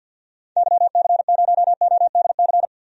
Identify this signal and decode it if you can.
Morse code, heard as a keyed tone, about 49 wpm, 700 Hz. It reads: YC0ODC
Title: DXCC-100 Call Signs - 6 Characters - 50wpm